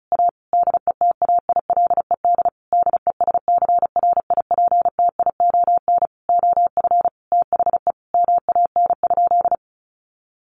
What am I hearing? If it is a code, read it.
Morse code, heard as a keyed tone, about 34 wpm, 700 Hz. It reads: A DETAILED DESCRIPTION OF THE MAN?
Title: Sentences from Top 1500 Words 35wpm